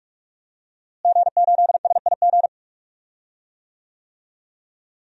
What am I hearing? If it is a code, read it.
Morse code, heard as a keyed tone, about 48 wpm, 700 Hz. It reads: G8SIG
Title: DXCC-100 Call Signs - 5 Characters - Repeated 3x - 45wpm